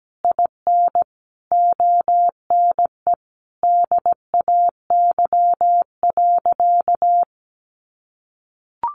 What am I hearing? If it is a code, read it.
Morse code, heard as a keyed tone, about 17 wpm, 700 Hz. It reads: IN ONE DAY.
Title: Sentences from Top 500 Words 17wpm